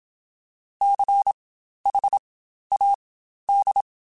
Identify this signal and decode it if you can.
Morse code, heard as a keyed tone, about 27 wpm, spaced as for 12 wpm, 800 Hz. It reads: CHAD